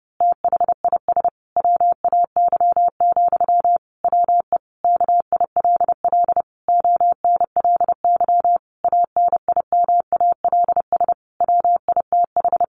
Morse code, heard as a keyed tone, about 30 wpm, 700 Hz. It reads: THIS WAY, WE KILL ONLY ANIMALS WITH
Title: A Sound Of Thunder 30wpm